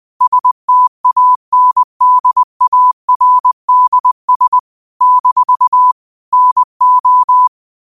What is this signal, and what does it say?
Morse code, heard as a keyed tone, about 20 wpm, 1000 Hz. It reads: STANDARDS - NO